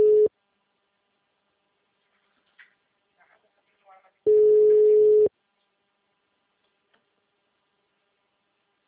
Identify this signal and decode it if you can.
Ringback tone (Europe, 425 Hz).
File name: out- -2002- - - .96709